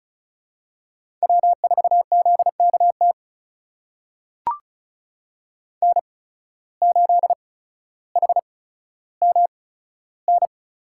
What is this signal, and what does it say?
Morse code, heard as a keyed tone, about 36 wpm, spaced as for 15 wpm, 700 Hz. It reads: W4ZKT N8HMN
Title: USA Call Signs - 8x Character Spacing - 35wpm